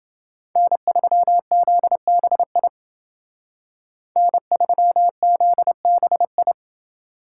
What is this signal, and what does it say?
Morse code, heard as a keyed tone, about 28 wpm, 700 Hz. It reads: N3ZBI N3ZBI